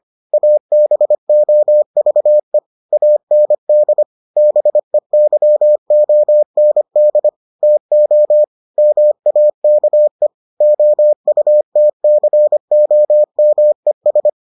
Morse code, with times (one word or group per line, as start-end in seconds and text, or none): 0.3-2.6 ABOVE
2.9-4.0 AND
4.4-7.3 BEYOND
7.6-8.4 TO
8.8-10.3 MAKE
10.6-14.3 OUTCOMES